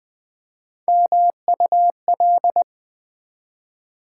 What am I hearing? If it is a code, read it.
Morse code, heard as a keyed tone, about 20 wpm, 700 Hz. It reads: MUL
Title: Build A Word - 3 Letters At A Time - 8-Letter Words - 20wpm